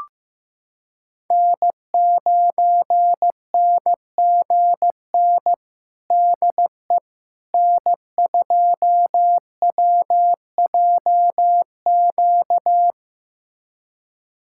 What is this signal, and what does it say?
Morse code, heard as a keyed tone, about 15 wpm, 700 Hz. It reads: N9NGN DE N2WJQ
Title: USA Callsign DE Callsign 15wpm